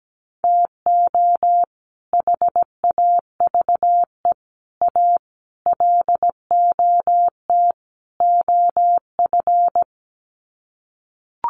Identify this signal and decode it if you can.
Morse code, heard as a keyed tone, about 17 wpm, 700 Hz. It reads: TO HAVE A LOT OF
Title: Sets of 5 Words Encore 17wpm